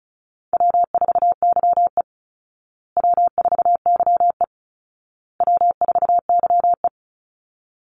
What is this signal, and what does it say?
Morse code, heard as a keyed tone, about 34 wpm, 700 Hz. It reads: W4YE W4YE W4YE
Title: Call Signs Repeated 2x - 35wpm